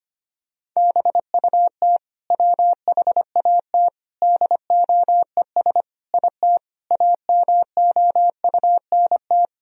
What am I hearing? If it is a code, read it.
Morse code, heard as a keyed tone, about 25 wpm, 700 Hz. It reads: BUT WHAT DOES IT AMOUNT
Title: Sentences from Top 1000 Words - Review Part 4 - 25wpm